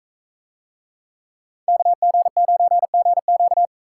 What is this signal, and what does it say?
Morse code, heard as a keyed tone, about 42 wpm, 700 Hz. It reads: KG9GQ